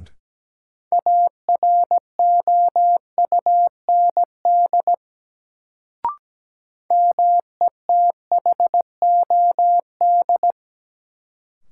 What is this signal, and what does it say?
Morse code, heard as a keyed tone, about 17 wpm, 700 Hz. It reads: AROUND METHOD
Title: Sentences from Top 600 Words - New Word Drill - 17wpm